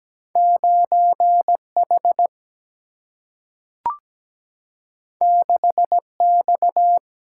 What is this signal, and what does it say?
Morse code, heard as a keyed tone, about 17 wpm, 700 Hz. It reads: 9H 6X
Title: Single-Letter Single-Number 17wpm